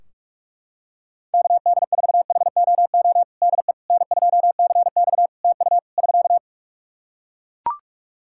Morse code, heard as a keyed tone, about 44 wpm, 700 Hz. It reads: KD4HOO DE N2CX TU <SK>
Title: Practice QSO - Exchange 4 - Ending - 45wpm